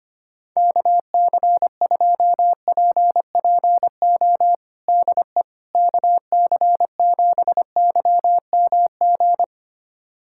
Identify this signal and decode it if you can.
Morse code, heard as a keyed tone, about 25 wpm, 700 Hz. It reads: KC2PPO DE KC7YMG
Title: USA Callsign DE Callsign - 6 Characters - 25wpm